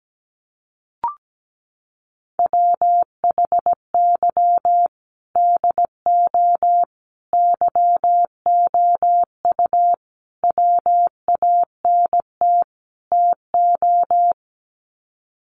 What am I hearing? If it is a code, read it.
Morse code, heard as a keyed tone, about 17 wpm, 700 Hz. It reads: WHY DO YOU WANT TO
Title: Sets of 5 Words Encore2 17wpm